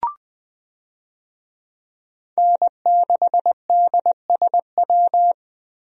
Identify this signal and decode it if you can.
Morse code, heard as a keyed tone, about 20 wpm, 700 Hz. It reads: N6DSW
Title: USA Call Signs - 5 Characters - Repeated 3x - 20wpm